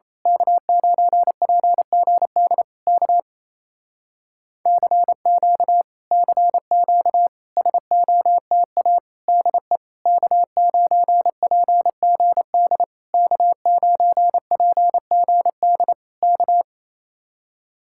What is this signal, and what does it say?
Morse code, heard as a keyed tone, about 27 wpm, 700 Hz. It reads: K9PGD K CQ CQ SOTA DE K9PGD K9PGD K
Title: Speed-Racing Common QSO Phrases Encore - 28wpm